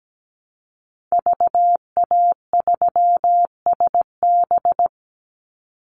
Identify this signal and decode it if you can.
Morse code, heard as a keyed tone, about 17 wpm, 700 Hz. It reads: VA3SB
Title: Call Signs Repeated 2x - 17wpm